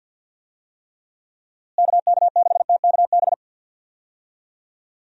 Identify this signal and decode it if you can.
Morse code, heard as a keyed tone, about 50 wpm, 700 Hz. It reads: KK6TKB
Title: USA Call Signs - 6 Characters - Repeated 3x - 50wpm